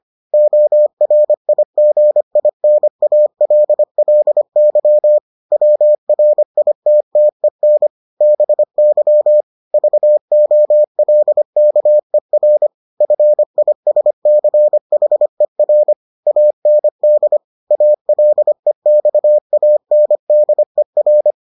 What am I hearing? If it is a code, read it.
Morse code, heard as a keyed tone, about 25 wpm, 600 Hz. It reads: ORIGINALLY WRITTEN BY VOLKER FISCHER AND ALEXANDER